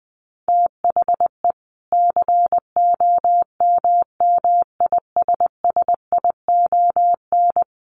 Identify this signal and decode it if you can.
Morse code, heard as a keyed tone, about 20 wpm, 700 Hz. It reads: THE COMMISSION
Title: Sentences from Top 1400 Words 20wpm